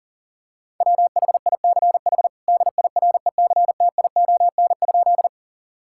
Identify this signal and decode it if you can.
Morse code, heard as a keyed tone, about 37 wpm, 700 Hz. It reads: WHICH DIRECTION?
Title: Sentences from Top 1000 Words - Review Part 5 - 40wpm